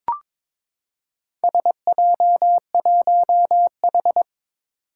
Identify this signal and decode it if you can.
Morse code, heard as a keyed tone, about 22 wpm, 700 Hz. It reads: SJ1H
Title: Four Challenging Letter-Number Combinations 22wpm